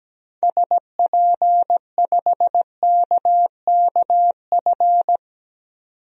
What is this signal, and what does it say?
Morse code, heard as a keyed tone, about 17 wpm, 700 Hz. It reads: SP5KKF
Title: DXCC-100 Call Signs - 6 Characters - 17wpm